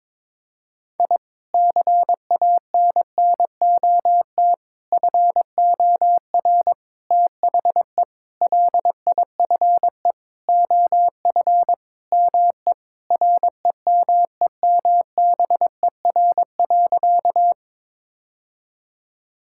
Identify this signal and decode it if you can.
Morse code, heard as a keyed tone, about 22 wpm, 700 Hz. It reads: I CANNOT FOR THE LIFE OF ME REMEMBER.
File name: practice / 22wpm / Sentences with 8-Word Phrases - 22wpm